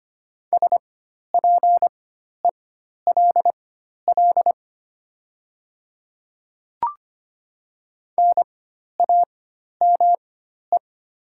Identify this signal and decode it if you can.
Morse code, heard as a keyed tone, about 25 wpm, spaced as for 12 wpm, 700 Hz. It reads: SPELL NAME
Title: 4x Character Spacing - Top 500 Words - 25wpm